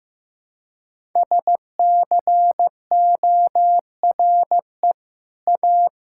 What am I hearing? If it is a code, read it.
Morse code, heard as a keyed tone, about 15 wpm, 700 Hz. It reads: SCORE A
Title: Sentences from Top 1600 Words 15wpm